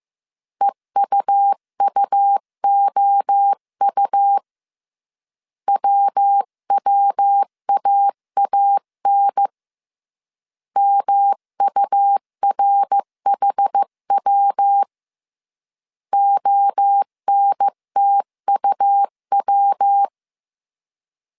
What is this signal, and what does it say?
Morse code, heard as a keyed tone, about 15 wpm, 800 Hz. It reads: EUUOU WWAAN MURHW ONTUW
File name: CW / DET / 1 3 1500C S A 14wpm 2447